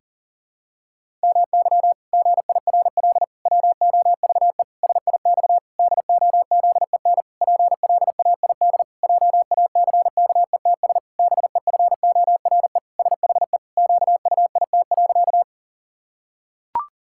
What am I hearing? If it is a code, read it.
Morse code, heard as a keyed tone, about 40 wpm, 700 Hz. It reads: MY GIRL WOVE SIX DOZEN PLAID JACKETS BEFORE SHE QUIT.
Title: Pangram Challenge v2 - 40wpm